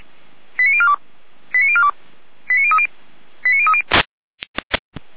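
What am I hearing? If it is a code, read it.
Five-tone selective call (ZVEI), heard as quick runs of five tones, tones of about 70 ms. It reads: 89042 89042 89030 89030
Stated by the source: ZVEI five-tone selective call 89229 received on an SDR.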